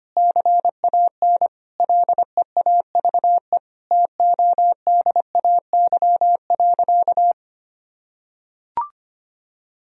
Morse code, heard as a keyed tone, about 25 wpm, 700 Hz. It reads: CAN LEAVE TODAY.
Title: Sentences from Top 1000 Words - Review Part 3 - 25wpm